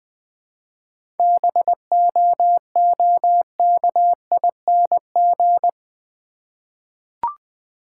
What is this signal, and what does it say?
Morse code, heard as a keyed tone, about 20 wpm, 700 Hz. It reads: BOOKING
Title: 7 Letter Words - First 3 Letters Given - 20wpm